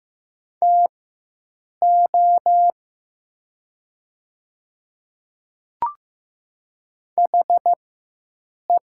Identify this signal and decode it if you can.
Morse code, heard as a keyed tone, about 15 wpm, spaced as for 7 wpm, 700 Hz. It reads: TO HE
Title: 4x Character Spacing - Top 200 Words - 15wpm